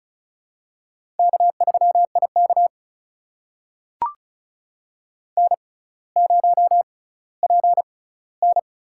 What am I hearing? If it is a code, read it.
Morse code, heard as a keyed tone, about 36 wpm, spaced as for 18 wpm, 700 Hz. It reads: K3IK N0PN
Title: USA Call Signs - 4 Characters - 6x Character Spacing - 35wpm